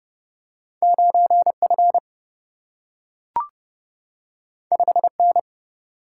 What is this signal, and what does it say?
Morse code, heard as a keyed tone, about 30 wpm, 700 Hz. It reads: 9F 5N
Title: Single-Letter Single-Number 30wpm